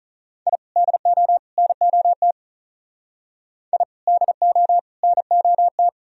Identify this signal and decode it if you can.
Morse code, heard as a keyed tone, about 38 wpm, 700 Hz. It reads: I DO NOT I DO NOT
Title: Speed-Racing - Sets of 3 Words 35wpm